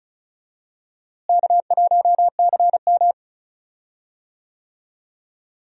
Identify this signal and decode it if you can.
Morse code, heard as a keyed tone, about 36 wpm, 700 Hz. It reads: K1CM
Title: USA Call Signs - 4 Characters - Repeated 2x - 35wpm